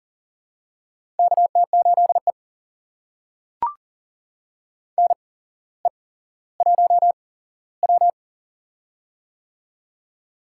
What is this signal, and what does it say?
Morse code, heard as a keyed tone, about 40 wpm, spaced as for 17 wpm, 700 Hz. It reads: KT8E NE1W
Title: USA Call Signs - 4 Characters - 8x Character Spacing - 40wpm